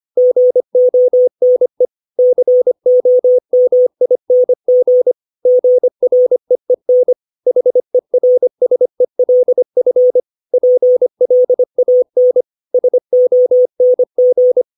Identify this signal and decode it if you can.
Morse code, heard as a keyed tone, about 25 wpm, 500 Hz. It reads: GONE COMING GREEN HERSELF PLAN SONG
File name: 632 words / 632 Words 25WPM